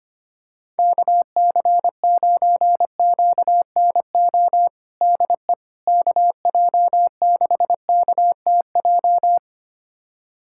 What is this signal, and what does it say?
Morse code, heard as a keyed tone, about 25 wpm, 700 Hz. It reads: KC9QNO DE KJ6KTJ
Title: USA Callsign DE Callsign 25wpm